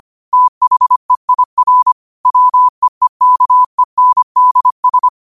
Morse code, heard as a keyed tone, about 25 wpm, 1000 Hz. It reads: THEIR WEEKENDS